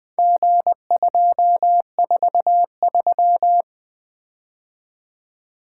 Morse code, heard as a keyed tone, about 20 wpm, 700 Hz. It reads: G243
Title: Four Letter-Number Combinations - 20wpm